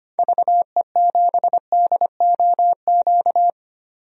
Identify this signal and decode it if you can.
Morse code, heard as a keyed tone, about 25 wpm, 700 Hz. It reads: VE7DOQ
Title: DXCC-100 Call Signs - 6 Characters - 25wpm